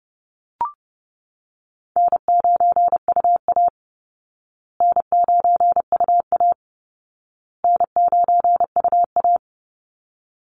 Morse code, heard as a keyed tone, about 30 wpm, 700 Hz. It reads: N9UA N9UA N9UA
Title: Easy Call Signs Repeated 3x - 30wpm